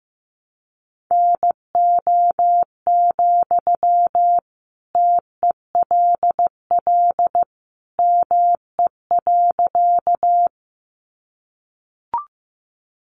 Morse code, heard as a keyed tone, about 15 wpm, 700 Hz. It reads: NO, TELL ME.
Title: Sentences from Top 200 Words 15wpm